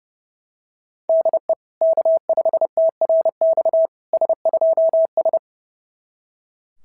Morse code, heard as a keyed tone, about 30 wpm, 690 Hz. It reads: DE K5TRX S2S
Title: SOTA QSO Practice 30wpm